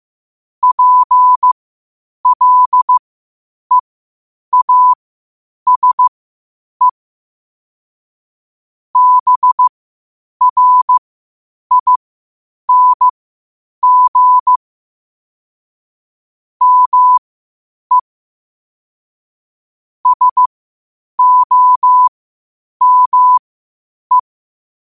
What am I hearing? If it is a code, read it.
Morse code, heard as a keyed tone, about 15 wpm, spaced as for 8 wpm, 1000 Hz. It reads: PLEASE BRING ME SOME
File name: qotdfiles / 8wpm